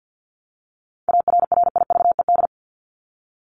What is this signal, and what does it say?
Morse code, heard as a keyed tone, about 49 wpm, 700 Hz. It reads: ARRIVED
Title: ED Suffix - Top 100 Words with Base Words 50wpm